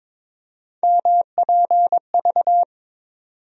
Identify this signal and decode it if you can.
Morse code, heard as a keyed tone, about 22 wpm, 700 Hz. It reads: MPV
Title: USA Call Sign Postfixes - 3 Letters 22wpm